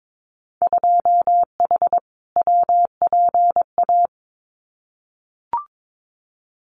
Morse code, heard as a keyed tone, about 22 wpm, 700 Hz. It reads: 2H WPA
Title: Winter Field Day Exchanges 22wpm